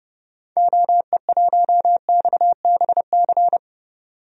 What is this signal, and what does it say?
Morse code, heard as a keyed tone, about 30 wpm, 700 Hz. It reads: OE1XBC
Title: DXCC-100 Call Signs - 6 Characters - 30wpm